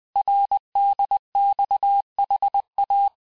Morse code, heard as a keyed tone, about 20 wpm, 800 Hz. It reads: RDXHA
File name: cours-858-texte-chiffre-vitesse-19mpm